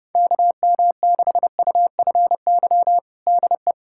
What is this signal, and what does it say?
Morse code, heard as a keyed tone, about 32 wpm, 700 Hz. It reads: KM6UFY DE
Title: USA Callsign DE Callsign 30wpm